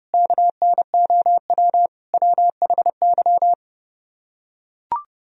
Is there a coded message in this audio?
Yes: Morse code from 0.1 to 3.5 s, heard as a keyed tone, about 30 wpm, 700 Hz. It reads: KNOW WHY